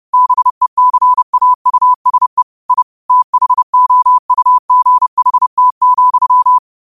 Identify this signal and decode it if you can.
Morse code, heard as a keyed tone, about 30 wpm, 1000 Hz. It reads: BECAUSE I THOUGHT,
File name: qotdfiles / 30wpm